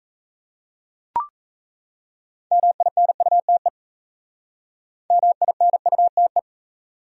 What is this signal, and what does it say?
Morse code, heard as a keyed tone, about 33 wpm, 700 Hz. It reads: MINUTE MINUTE
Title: Speed-Racing Top 500 Words 28wpm